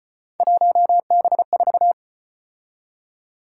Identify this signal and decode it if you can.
Morse code, heard as a keyed tone, about 34 wpm, 700 Hz. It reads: 1B4